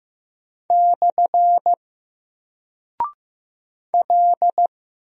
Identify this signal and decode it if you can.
Morse code, heard as a keyed tone, about 15 wpm, 700 Hz. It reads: / L
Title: Single Character (Letter:Number:Punctuation:Prosign) 15wpm